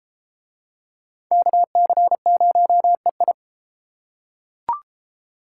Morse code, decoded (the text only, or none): KC0EI